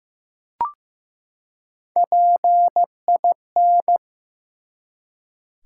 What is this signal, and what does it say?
Morse code, heard as a keyed tone, about 15 wpm, 700 Hz. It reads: PIN
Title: Three Letter Words - 15wpm